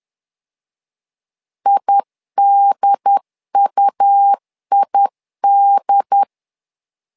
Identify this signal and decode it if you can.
Morse code, heard as a keyed tone, about 10 wpm, 800 Hz. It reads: IDUID
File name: LEZIONI / 2 UHDN / 2 UHDN 10wpm 1904